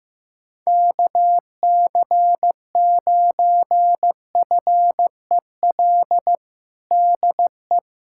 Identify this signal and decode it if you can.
Morse code, heard as a keyed tone, about 15 wpm, 700 Hz. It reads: KC9FEL DE
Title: USA Callsign DE Callsign - 6 Characters - 15wpm